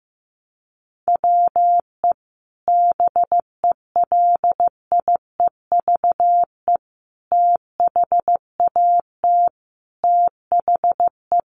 Morse code, heard as a keyed tone, about 15 wpm, 700 Hz. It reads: WE BELIEVE THAT THE
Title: Sentences from Top 1300 Words 15wpm